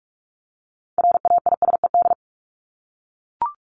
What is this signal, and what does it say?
Morse code, heard as a keyed tone, about 44 wpm, 700 Hz. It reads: RAISED
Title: ED Suffix - Top 100 Words with Base Words 45wpm